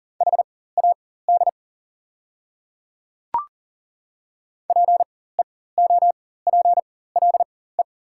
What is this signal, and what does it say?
Morse code, heard as a keyed tone, about 40 wpm, spaced as for 19 wpm, 700 Hz. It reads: HAD PEOPLE
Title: 4x Character Spacing - Top 100 Words - 40wpm